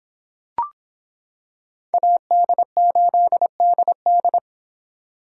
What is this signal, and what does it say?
Morse code, heard as a keyed tone, about 26 wpm, 700 Hz. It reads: AD8DD